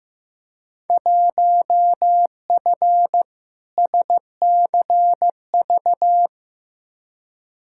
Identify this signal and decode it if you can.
Morse code, heard as a keyed tone, about 15 wpm, 700 Hz. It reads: 1F SCV
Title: Field Day Exchanges Encore - 15wpm